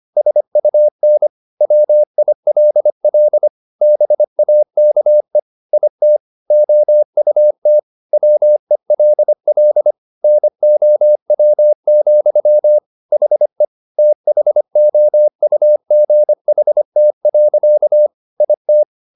Morse code, heard as a keyed tone, about 25 wpm, 600 Hz. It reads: SUN WILL BAKE IT OUT WELL NOW, HE THOUGHT. IT